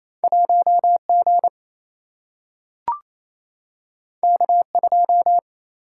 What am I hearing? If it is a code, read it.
Morse code, heard as a keyed tone, about 28 wpm, 700 Hz. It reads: 1G K2
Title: Single-Letter Single-Number 28wpm